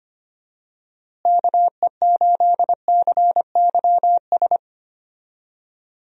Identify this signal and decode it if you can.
Morse code, heard as a keyed tone, about 25 wpm, 700 Hz. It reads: KE8CYS
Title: USA Call Signs - 6 Characters - Repeated 2x - 25wpm